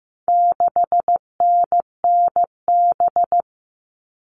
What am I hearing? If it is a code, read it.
Morse code, heard as a keyed tone, about 15 wpm, 700 Hz. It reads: 6NNB
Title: All Cut Numbers - 4-Digit 15wpm